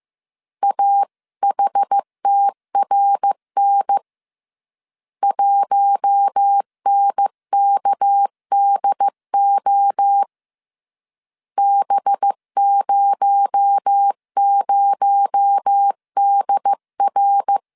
Morse code, heard as a keyed tone, about 15 wpm, 800 Hz. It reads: AHTRN 1NKDO B00DR
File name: CW / DET / 1 6 1500C S B 14wpm 2958